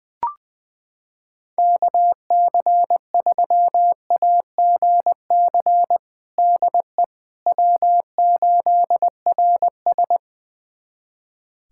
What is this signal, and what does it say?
Morse code, heard as a keyed tone, about 20 wpm, 700 Hz. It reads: KC3AGC DE W8RS
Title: USA Callsign DE Callsign 20wpm